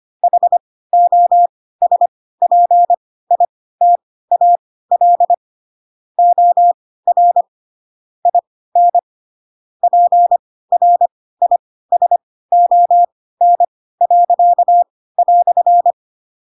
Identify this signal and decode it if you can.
Morse code, heard as a keyed tone, about 25 wpm, spaced as for 16 wpm, 700 Hz. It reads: HOSPITAL OR IN PRISON."
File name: Pinocchio / Pinocchio ch4 10 25 ik1hge